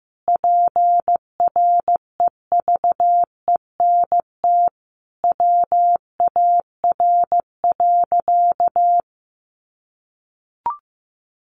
Morse code, heard as a keyed tone, about 14 wpm, 700 Hz. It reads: PREVENT WAR.
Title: Sentences from Top 1100 Words 15wpm